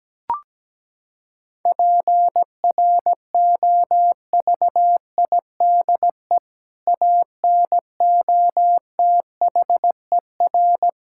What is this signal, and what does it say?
Morse code, heard as a keyed tone, about 17 wpm, 700 Hz. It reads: PROVIDE ANOTHER
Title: Sentences from Top 1000 Words - Review Part 5 - 17wpm